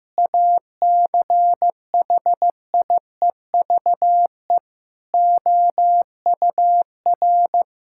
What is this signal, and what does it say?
Morse code, heard as a keyed tone, about 14 wpm, 700 Hz. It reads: ACHIEVE OUR
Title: Sentences from Top 1000 Words - Review Part 6 - 15wpm